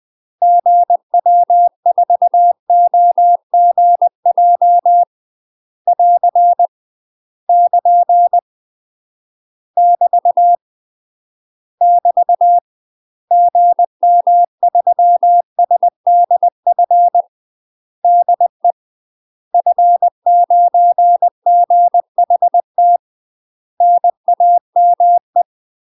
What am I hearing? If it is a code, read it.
Morse code, heard as a keyed tone, about 20 wpm, 700 Hz. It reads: GW4OGJ <AR> <KN> <BT> <BT> GM3SDF DE F9GHT NAME
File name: UK MASTER QSOs TRAINING / 20WPM / UK MASTER QSOs 0030